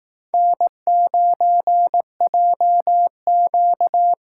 Morse code, heard as a keyed tone, about 18 wpm, 700 Hz. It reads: N9JQ
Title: Speed Racing - USA 1x2 Call Signs 15wpm